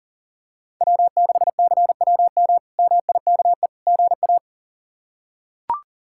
Morse code, heard as a keyed tone, about 40 wpm, 700 Hz. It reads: W6CWM MIKE GA
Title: K1USN SST QSO Practice - 40wpm